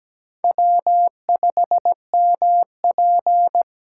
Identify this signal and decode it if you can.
Morse code, heard as a keyed tone, about 17 wpm, 700 Hz. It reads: W5MP
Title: USA Call Signs - 8x Character Spacing - 17wpm